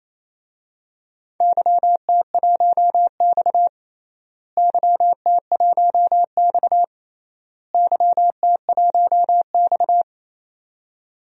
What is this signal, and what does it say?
Morse code, heard as a keyed tone, about 28 wpm, 700 Hz. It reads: YT1X YT1X YT1X
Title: DXCC-100 Call Signs - 4 Characters - Repeated 3x - 28wpm